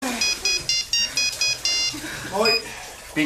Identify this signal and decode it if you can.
Speech. There is no code.